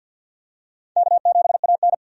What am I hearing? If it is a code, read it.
Morse code, heard as a keyed tone, about 50 wpm, 700 Hz. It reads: K7AN
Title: USA Call Signs - 4 Characters - 50wpm